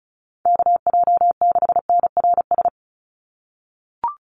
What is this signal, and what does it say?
Morse code, heard as a keyed tone, about 34 wpm, 700 Hz. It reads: KJ6NRS
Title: USA Call Signs 35wpm